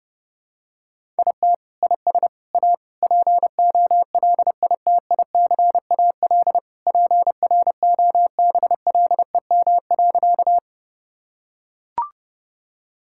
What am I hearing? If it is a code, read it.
Morse code, heard as a keyed tone, about 30 wpm, 700 Hz. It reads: IT IS A POLITICAL PROBLEM.
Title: Sentences from Top 1000 Words - Review Part 5 - 30wpm